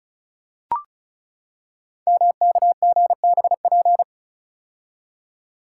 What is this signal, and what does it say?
Morse code, heard as a keyed tone, about 36 wpm, 700 Hz. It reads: MKGBP
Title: Code Groups - 5 Letters - 35wpm